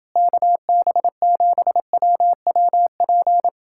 Morse code, heard as a keyed tone, about 31 wpm, 700 Hz. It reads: KB7WWP